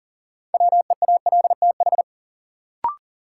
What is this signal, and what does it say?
Morse code, heard as a keyed tone, about 40 wpm, 700 Hz. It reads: WEALTH
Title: Sentences from Top 1000 Words - Review Part 4 - New Word Drill - 40wpm